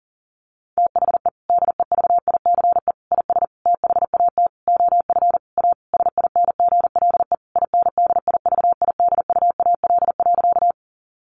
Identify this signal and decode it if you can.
Morse code, heard as a keyed tone, about 40 wpm, 700 Hz. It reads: THE DEVICE IS THAT OF A SINGLE INDIVIDUAL.
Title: Sentences from Top 1500 Words 40wpm